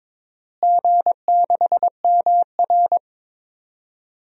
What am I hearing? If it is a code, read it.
Morse code, heard as a keyed tone, about 22 wpm, 700 Hz. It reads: G6MR
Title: DXCC-100 Call Signs - 4 Characters - Repeated 2x - 22wpm